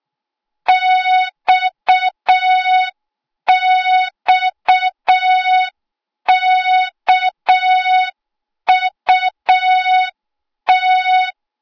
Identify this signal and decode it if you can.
Morse code, heard as a keyed tone, about 6 wpm, 3700 Hz. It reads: XXKUT
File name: Morze-HA2MN / hang / 08KXC HA2MN